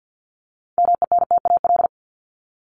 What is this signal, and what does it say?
Morse code, heard as a keyed tone, about 50 wpm, 700 Hz. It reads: MENTAL
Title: Top 501-750 Words 50wpm